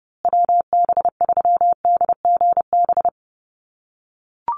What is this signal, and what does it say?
Morse code, heard as a keyed tone, about 30 wpm, 700 Hz. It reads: WB3DGB
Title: USA Call Signs Repeated 3x 30wpm